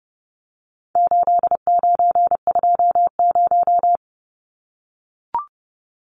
Morse code, heard as a keyed tone, about 30 wpm, 700 Hz. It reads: 8920
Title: Numbers - 4-Digit 30wpm